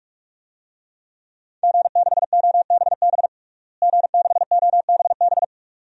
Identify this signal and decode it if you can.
Morse code, heard as a keyed tone, about 46 wpm, 700 Hz. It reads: G6OBB G6OBB
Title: DXCC-100 Call Signs - 5 Characters - Repeated 2x - 45wpm